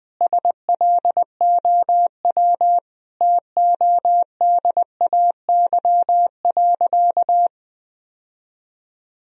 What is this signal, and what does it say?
Morse code, heard as a keyed tone, about 20 wpm, 700 Hz. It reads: SLOW TODAY.
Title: Sentences from Top 1000 Words - Review Part 2 - 20wpm